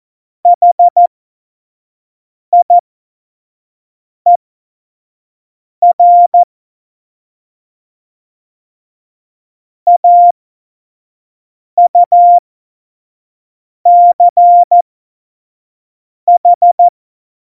Morse code, heard as a keyed tone, about 14 wpm, spaced as for 5 wpm, 700 Hz. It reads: HIER AUCH
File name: Morsekurs / Witze / Witz 08. September 2020, 25 F70